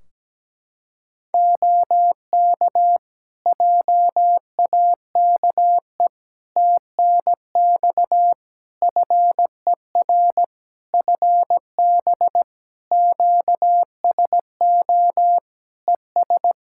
Morse code, heard as a keyed tone, about 17 wpm, 700 Hz. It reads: OK JAKE TNX FER FB QSO ES
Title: Practice QSO - Exchange 4 - Ending - 17wpm